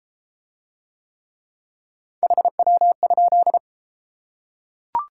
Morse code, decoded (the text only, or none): HW?